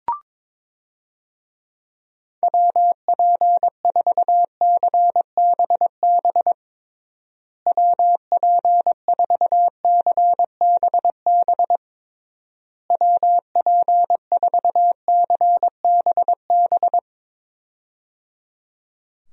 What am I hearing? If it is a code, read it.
Morse code, heard as a keyed tone, about 22 wpm, 700 Hz. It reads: WP4CBB WP4CBB WP4CBB
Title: USA Call Signs - 6 Characters - Repeated 3x - 22wpm